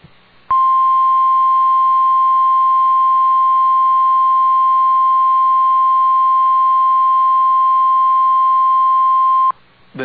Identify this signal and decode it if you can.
Emergency Alert System (SAME) tones (the 1050 Hz attention tone). It is heard from 0.4 to 9.6 s.